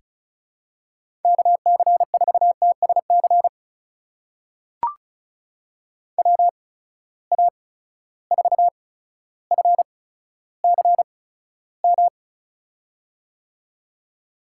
Morse code, heard as a keyed tone, about 36 wpm, spaced as for 15 wpm, 700 Hz. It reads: KC4TSC WA4FCM
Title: USA Call Signs - 6 Characters - 8x Character Spacing - 35wpm